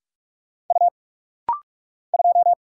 Morse code, heard as a keyed tone, about 46 wpm, 700 Hz. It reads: U 2
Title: Single Letter-Number Alt - Rapid-Fire 45wpm